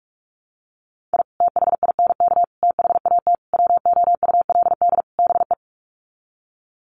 Morse code, heard as a keyed tone, about 44 wpm, 700 Hz. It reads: I THINK THAT WOULD BE
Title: Sets of 5 Words Encore2 45wpm